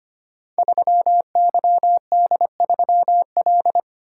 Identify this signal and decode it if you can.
Morse code, heard as a keyed tone, about 25 wpm, 700 Hz. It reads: 3YD3L
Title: Code Groups - 5 Letter:Numbers - 25wpm